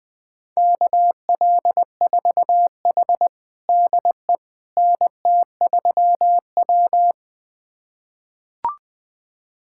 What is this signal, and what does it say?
Morse code, heard as a keyed tone, about 20 wpm, 700 Hz. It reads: KL4H DE NT3W
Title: USA Callsign DE Callsign - 4 Characters - 20wpm